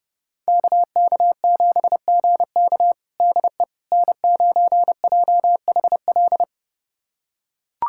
Morse code, heard as a keyed tone, about 31 wpm, 700 Hz. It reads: KK7GK DE N9JHL
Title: USA Callsign DE Callsign - 5 Characters - 30wpm